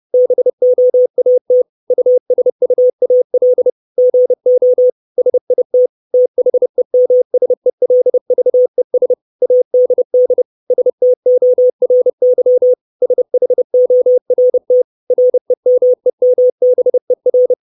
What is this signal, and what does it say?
Morse code, heard as a keyed tone, about 30 wpm, 500 Hz. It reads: BOAT USUAL GO SIT THEMSELVES ADD STORY SHORT REMEMBER
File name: 632 Words 30WPM